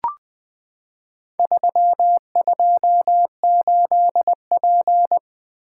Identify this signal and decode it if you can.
Morse code, heard as a keyed tone, about 20 wpm, 700 Hz. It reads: 328P
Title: Four Letter-Number Combinations - 20wpm